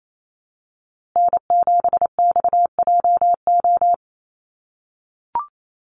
Morse code, heard as a keyed tone, about 28 wpm, 700 Hz. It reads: N7XJO
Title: USA Call Signs Repeated 2x 28wpm